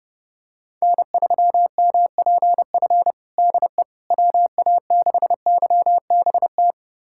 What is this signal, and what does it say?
Morse code, heard as a keyed tone, about 30 wpm, 700 Hz. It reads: N3MPF DE WA6YBT
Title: USA Callsign DE Callsign 30wpm